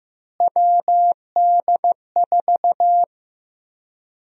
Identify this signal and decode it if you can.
Morse code, heard as a keyed tone, about 15 wpm, 700 Hz. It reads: WD4
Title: USA Call Sign Prefixes - Repeated 2x - 15wpm